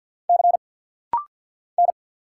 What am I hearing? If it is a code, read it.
Morse code, heard as a keyed tone, about 50 wpm, 700 Hz. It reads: C N
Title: Single Letters - Rapid-Fire 50wpm